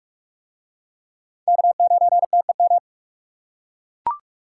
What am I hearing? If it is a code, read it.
Morse code, heard as a keyed tone, about 44 wpm, 700 Hz. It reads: K9TEM